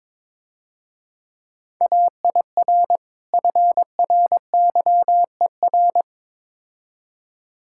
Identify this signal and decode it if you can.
Morse code, heard as a keyed tone, about 24 wpm, 690 Hz. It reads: AIR FRYER
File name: Easy Words - Speed-Racing with Hints - 22wpm